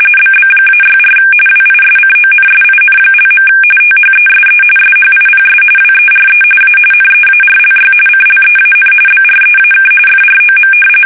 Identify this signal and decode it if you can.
RTTY, about 45.45 baud. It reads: BB-64 WISCONSIN). THESE SHIPS WERE A NEARLY CLEAN BREAK FROM
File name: fsk850 / battleships